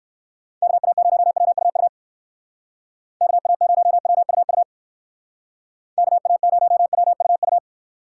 Morse code, heard as a keyed tone, about 59 wpm, 700 Hz. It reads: KA0WUU KA0WUU KA0WUU